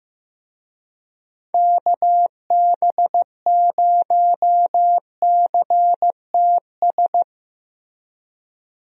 Morse code, heard as a keyed tone, about 15 wpm, 700 Hz. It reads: KB0CTS
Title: USA Call Signs - 6 Characters - 15wpm